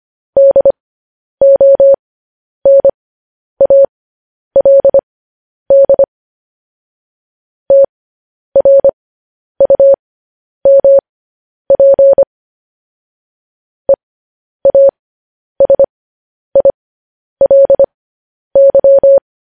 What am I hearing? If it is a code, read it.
Morse code, heard as a keyed tone, about 25 wpm, spaced as for 10 wpm, 550 Hz. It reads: DONALD TRUMP EASILY